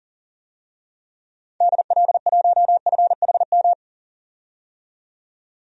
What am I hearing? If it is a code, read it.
Morse code, heard as a keyed tone, about 40 wpm, 700 Hz. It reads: DL1FHM
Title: DXCC-100 Call Signs - 6 Characters - 40wpm